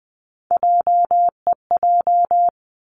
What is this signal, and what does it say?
Morse code, heard as a keyed tone, about 20 wpm, 700 Hz. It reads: JEJ
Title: Three Challenging Letter-Number Combinations 20wpm